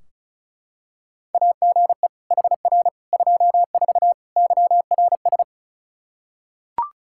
Morse code, heard as a keyed tone, about 36 wpm, 700 Hz. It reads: AGE HR 24 YRS
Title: Practice QSO - Exchange 3 - More Info - 35wpm